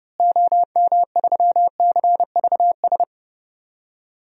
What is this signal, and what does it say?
Morse code, heard as a keyed tone, about 30 wpm, 700 Hz. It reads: OM3CVS
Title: DXCC-100 Call Signs - 6 Characters - Repeated 3x - 30wpm